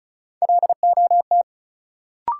Morse code, heard as a keyed tone, about 36 wpm, 700 Hz. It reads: LOT
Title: Sentences from Top 500 Words - New Word Drill - 35wpm